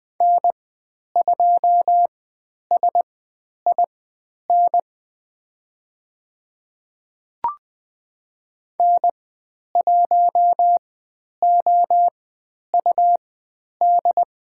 Morse code, heard as a keyed tone, about 20 wpm, spaced as for 10 wpm, 700 Hz. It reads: N2SIN N1OUD
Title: 102 Lesson 17g - Letter u - Call Signs 20wpm